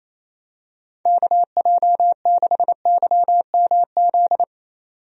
Morse code, heard as a keyed tone, about 27 wpm, 700 Hz. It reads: KJ6YMZ
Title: USA Call Signs - 6 Characters - 28wpm